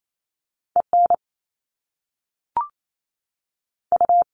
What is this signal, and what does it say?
Morse code, heard as a keyed tone, about 28 wpm, spaced as for 3 wpm, 700 Hz. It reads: E NU
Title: Top Two Letter Combinations in Words 28wpm